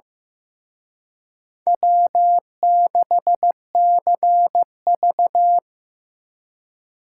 Morse code, heard as a keyed tone, about 18 wpm, 700 Hz. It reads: W6CV
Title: Speed Racing - USA Call Signs 15wpm